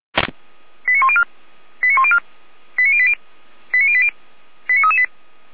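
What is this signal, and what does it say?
Five-tone selective call (ZVEI), heard as quick runs of five tones, tones of about 70 ms. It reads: 89104 89104 89080 89080 89338
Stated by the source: Five-tone ZVEI selective call 89280 received on an SDR.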